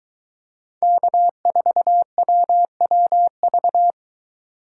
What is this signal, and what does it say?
Morse code, heard as a keyed tone, about 22 wpm, 700 Hz. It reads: K4WWV